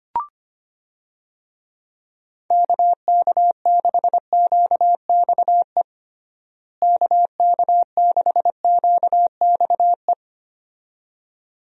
Morse code, heard as a keyed tone, about 25 wpm, 700 Hz. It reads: KK6QXE KK6QXE
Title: USA Call Signs - 6 Characters - Repeated 2x - 25wpm